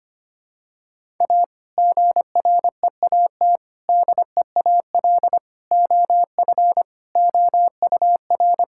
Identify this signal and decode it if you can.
Morse code, heard as a keyed tone, about 25 wpm, 700 Hz. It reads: A GREAT DEAL OF OUR
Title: Sentences from Top 1000 Words - Review Part 6 - 25wpm